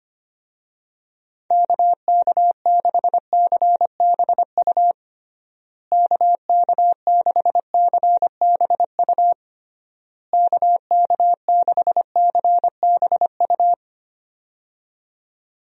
Morse code, heard as a keyed tone, about 25 wpm, 700 Hz. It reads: KK6CBU KK6CBU KK6CBU
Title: USA Call Signs - 6 Characters - Repeated 3x - 25wpm